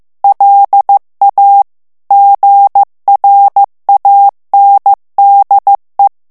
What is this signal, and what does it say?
Morse code, heard as a keyed tone, about 15 wpm, 800 Hz. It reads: LA GRANDE
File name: morse mp3 / FRANCAIS TEXTE CLAIR 2 VITESSE 14 MOTS MINUTES